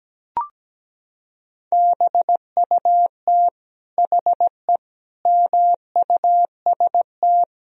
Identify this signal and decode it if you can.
Morse code, heard as a keyed tone, about 17 wpm, 700 Hz. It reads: BUT HE MUST
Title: Sentences from Top 1000 Words - Review Part 3 - 17wpm